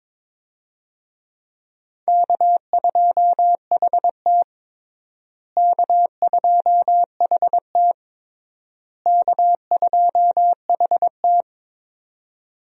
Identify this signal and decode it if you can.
Morse code, heard as a keyed tone, about 22 wpm, 700 Hz. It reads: K2HT K2HT K2HT
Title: USA Call Signs - 4 Characters - Repeated 3x - 22wpm